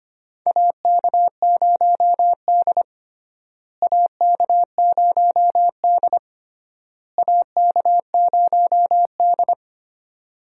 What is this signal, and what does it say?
Morse code, heard as a keyed tone, about 25 wpm, 700 Hz. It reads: AK0D AK0D AK0D
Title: USA Call Signs - 4 Characters - Repeated 3x - 25wpm